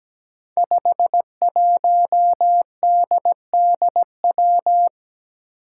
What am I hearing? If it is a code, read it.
Morse code, heard as a keyed tone, about 17 wpm, 700 Hz. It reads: 51DDW
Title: Code Groups - 5 Letter:Numbers - 17wpm